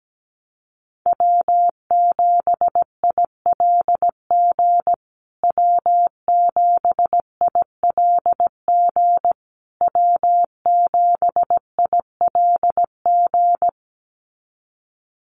Morse code, heard as a keyed tone, about 17 wpm, 700 Hz. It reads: W7ILG W7ILG W7ILG
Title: USA Call Signs Repeated 3x 17wpm